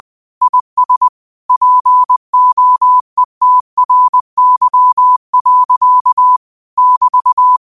Morse code, heard as a keyed tone, about 20 wpm, 1000 Hz. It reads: IS POETRY. <BT>
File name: qotdfiles / 20wpm